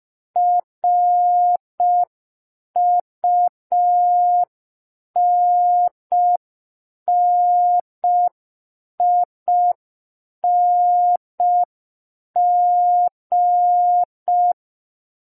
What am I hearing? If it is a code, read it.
Morse code, heard as a keyed tone, about 5 wpm, 710 Hz. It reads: RUNNING
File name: g3zpd / Morse code P4